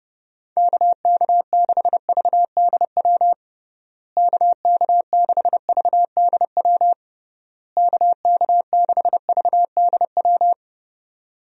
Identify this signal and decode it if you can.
Morse code, heard as a keyed tone, about 31 wpm, 700 Hz. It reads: KK6VDW KK6VDW KK6VDW
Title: USA Call Signs - 6 Characters - Repeated 3x - 30wpm